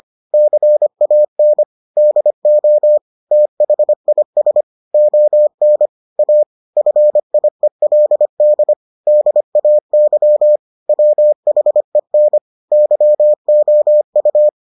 Morse code, heard as a keyed tone, about 25 wpm, 600 Hz. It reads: CAN DO THIS ON A FIELD DAY WHEN YOU